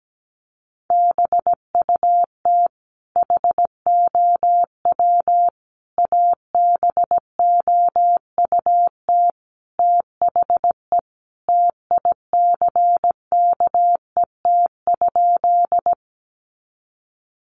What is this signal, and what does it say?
Morse code, heard as a keyed tone, about 17 wpm, 700 Hz. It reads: BUT HOW ABOUT THE TICKET?
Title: Sentences from Top 1400 Words 17wpm